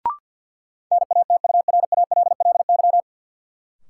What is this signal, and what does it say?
Morse code, heard as a keyed tone, about 53 wpm, 700 Hz. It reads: NATURALLY
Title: Sentences from Top 1300 Words - New Word Drill - 50wpm